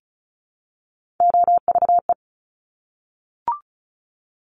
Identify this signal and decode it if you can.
Morse code, heard as a keyed tone, about 36 wpm, 700 Hz. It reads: OVE
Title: Top Three Letter Combinations in Words 35wpm